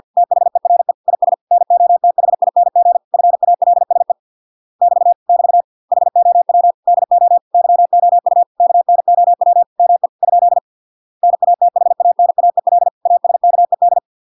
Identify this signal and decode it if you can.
Morse code, heard as a keyed tone, about 50 wpm, 700 Hz. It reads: THERE IS NOTHING FALSE <BT> <BT> HOW DO YOU KNOW ME ? NATHANAEL ASKED